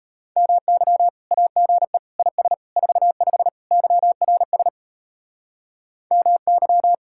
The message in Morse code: MY AGE IS 45 YRS MY

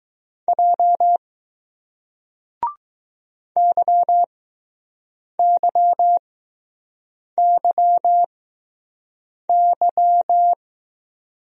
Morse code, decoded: J YYYY

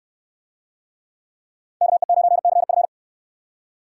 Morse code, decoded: KE9CF

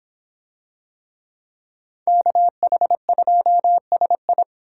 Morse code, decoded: KH2SI

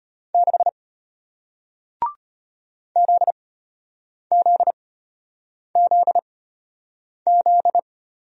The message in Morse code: 6 ZZZZ

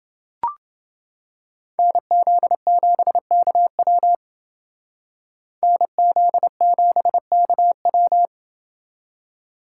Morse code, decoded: NZ7KW NZ7KW

